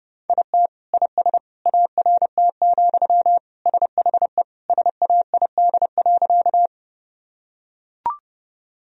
Morse code, heard as a keyed tone, about 30 wpm, 700 Hz. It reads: IT IS ART, SHE SAID.
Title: Sentences from Top 1000 Words - Review Part 3 - 30wpm